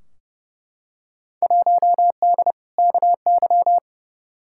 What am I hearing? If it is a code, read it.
Morse code, heard as a keyed tone, about 30 wpm, 700 Hz. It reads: 1D KY
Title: Field Day Exchanges - 30wpm